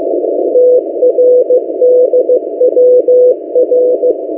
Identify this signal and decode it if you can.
Morse code, heard as a keyed tone, about 15 wpm, 500 Hz. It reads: TRDWR
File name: M08a- -1355utc-8096khz